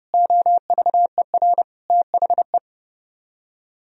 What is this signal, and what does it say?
Morse code, heard as a keyed tone, about 30 wpm, 700 Hz. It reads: OVER THE